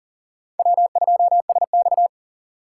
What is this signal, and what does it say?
Morse code, heard as a keyed tone, about 38 wpm, 700 Hz. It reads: W2SX
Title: USA Call Signs - 4 Characters - 6x Character Spacing - 40wpm